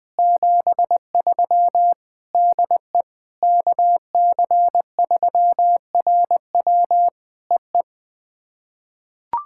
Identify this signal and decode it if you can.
Morse code, heard as a keyed tone, about 20 wpm, 700 Hz. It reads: 73 DE KC3RW EE
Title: POTA QSO Practice - 20wpm